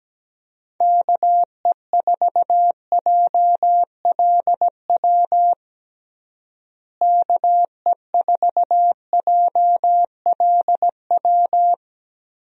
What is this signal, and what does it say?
Morse code, heard as a keyed tone, about 17 wpm, 700 Hz. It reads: KE4JLW KE4JLW
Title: USA Call Signs - 6 Characters - Repeated 2x - 17wpm